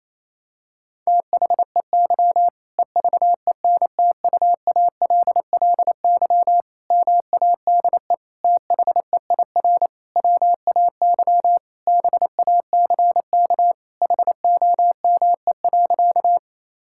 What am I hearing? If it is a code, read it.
Morse code, heard as a keyed tone, about 27 wpm, 700 Hz. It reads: THEY EVENTUALLY MADE THEIR WAY BACK HOME.
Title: Sentences from Top 1000 Words - Review Part 6 - 28wpm